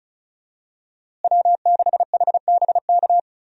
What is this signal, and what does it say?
Morse code, heard as a keyed tone, about 36 wpm, 700 Hz. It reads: W6HBK
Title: USA Call Signs - 5 Characters - Repeated 3x - 35wpm